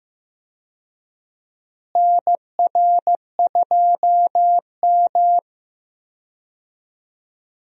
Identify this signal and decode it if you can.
Morse code, heard as a keyed tone, about 15 wpm, 700 Hz. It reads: NR2M
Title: USA Call Signs - 4 Characters - Repeated 3x - 15wpm